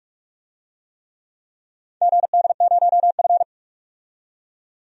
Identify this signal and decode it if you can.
Morse code, heard as a keyed tone, about 46 wpm, 700 Hz. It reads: GD0F
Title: DXCC-100 Call Signs - 4 Characters - 45wpm